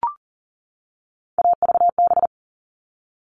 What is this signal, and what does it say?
Morse code, heard as a keyed tone, about 40 wpm, 700 Hz. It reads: AVB
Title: All Cut Numbers - 3-Digit 40wpm